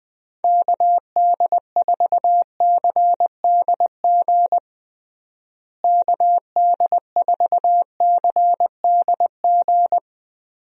Morse code, heard as a keyed tone, about 21 wpm, 700 Hz. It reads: KD4CDG KD4CDG
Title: USA Call Signs - 6 Characters - Repeated 2x - 20wpm